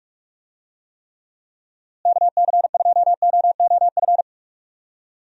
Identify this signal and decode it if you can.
Morse code, heard as a keyed tone, about 42 wpm, 700 Hz. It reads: KC2OOF
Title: USA Call Signs - 6 Characters - Repeated 3x - 45wpm